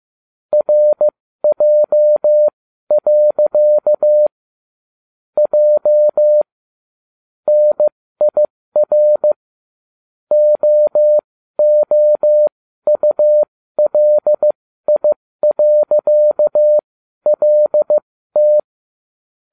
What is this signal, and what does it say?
Morse code, heard as a keyed tone, about 15 wpm, spaced as for 12 wpm, 600 Hz. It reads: RJ. J NIR OOULI.LT